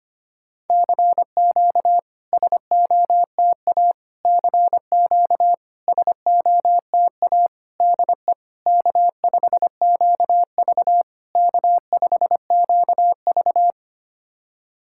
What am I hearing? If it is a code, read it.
Morse code, heard as a keyed tone, about 25 wpm, 700 Hz. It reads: CQ SOTA CQ SOTA DE K5QV K5QV
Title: SOTA QSO Practice 25wpm